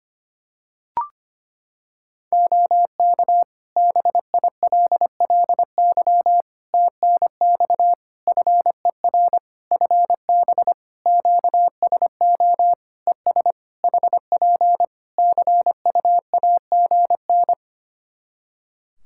Morse code, heard as a keyed tone, about 25 wpm, 700 Hz. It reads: OK BILLY TNX FER FB QSO ES HP CUAGN
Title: Practice QSO - Exchange 4 - Ending - 25wpm